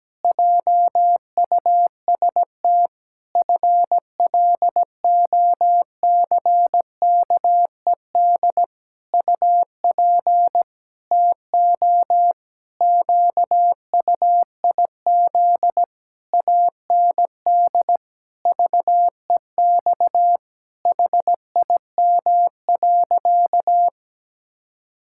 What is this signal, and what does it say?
Morse code, heard as a keyed tone, about 17 wpm, 700 Hz. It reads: JUST FLOCKED UP TO QUIZ AND VEX HIM.
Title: Pangram Challenge v2 - 17wpm